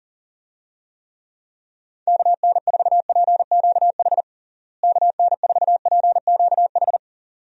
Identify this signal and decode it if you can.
Morse code, heard as a keyed tone, about 40 wpm, 700 Hz. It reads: KN4PQH KN4PQH
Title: USA Call Signs - 6 Characters - Repeated 2x - 40wpm